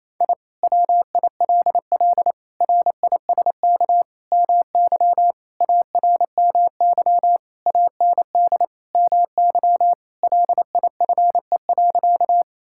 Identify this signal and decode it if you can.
Morse code, heard as a keyed tone, about 28 wpm, 700 Hz. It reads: I WILL RISK MY ARMY AND MY LIFE.
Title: Sentences from Top 1000 Words - Review Part 5 - 28wpm